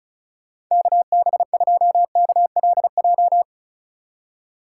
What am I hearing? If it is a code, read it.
Morse code, heard as a keyed tone, about 46 wpm, 700 Hz. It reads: KB2KLJ